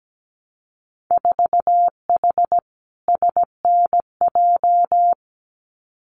Morse code, heard as a keyed tone, about 17 wpm, 700 Hz. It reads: 4H SNJ
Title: Winter Field Day Exchanges Encore 17wpm